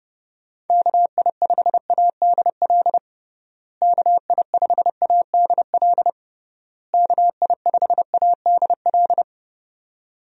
Morse code, heard as a keyed tone, about 30 wpm, 700 Hz. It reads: KI5ADL KI5ADL KI5ADL
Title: USA Call Signs - 6 Characters - Repeated 3x - 30wpm